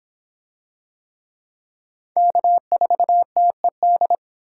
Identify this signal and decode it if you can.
Morse code, heard as a keyed tone, about 25 wpm, 700 Hz. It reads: K4TED